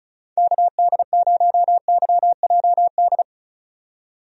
Morse code, heard as a keyed tone, about 36 wpm, 700 Hz. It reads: KD0YJD
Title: USA Call Signs - 6 Characters - 35wpm